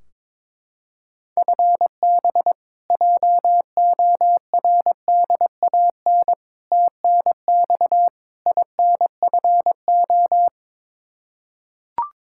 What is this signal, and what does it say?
Morse code, heard as a keyed tone, about 21 wpm, 700 Hz. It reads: FB JORDAN TNX INFO
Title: Practice QSO - Exchange 3 - More Info - 22wpm